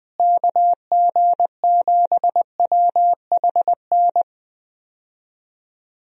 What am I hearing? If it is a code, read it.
Morse code, heard as a keyed tone, about 21 wpm, 700 Hz. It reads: KG7WHN
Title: USA Call Signs - 6 Characters - Repeated 2x - 20wpm